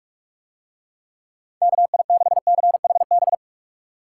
Morse code, heard as a keyed tone, about 45 wpm, 700 Hz. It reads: KI6CHB